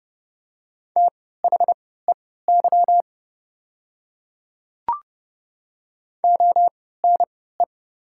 Morse code, heard as a keyed tone, about 30 wpm, spaced as for 17 wpm, 700 Hz. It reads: THEY ONE
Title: 3x Character Spacing - Top 100 Words - 30wpm